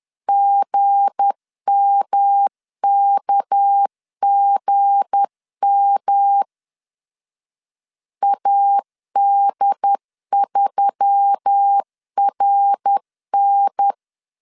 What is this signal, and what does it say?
Morse code, heard as a keyed tone, about 10 wpm, 800 Hz. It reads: GMKGM AD3RN
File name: LEZIONI / 7 GF9Z / 7 G 10wpm 2501